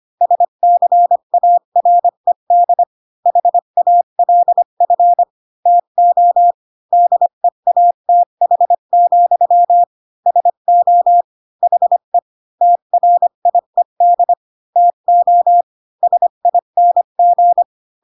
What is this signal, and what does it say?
Morse code, heard as a keyed tone, about 25 wpm, 700 Hz. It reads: SCARED HALF TO DEATH, SO HE TRIED TO SING
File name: Pinocchio / Pinocchio ch1 20 25 ik1hge